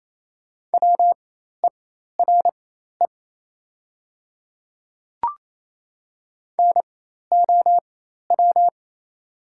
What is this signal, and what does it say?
Morse code, heard as a keyed tone, about 27 wpm, spaced as for 13 wpm, 700 Hz. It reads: WERE NOW
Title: 4x Character Spacing - Top 100 Words - 28wpm